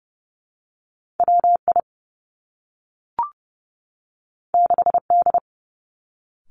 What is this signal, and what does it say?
Morse code, heard as a keyed tone, about 30 wpm, 700 Hz. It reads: W I 6 D
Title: Two Challenging Letter-Number Combinations 30wpm